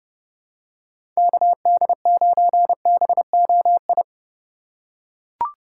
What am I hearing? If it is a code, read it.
Morse code, heard as a keyed tone, about 30 wpm, 700 Hz. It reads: KD9BOI
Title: USA Call Signs - 6 Characters - 4x Character Spacing - 30wpm